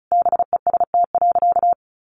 Morse code, heard as a keyed tone, about 34 wpm, 700 Hz. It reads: BEST.
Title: Sentences from Top 500 Words 35wpm